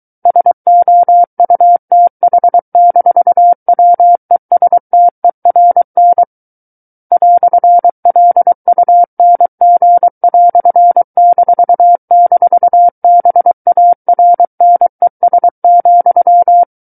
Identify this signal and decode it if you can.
Morse code, heard as a keyed tone, about 23 wpm, 700 Hz. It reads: SOUTH-WESTERN "LUNG"--BARNES,
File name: Book One / 14-Chapter 14 - 23 WPM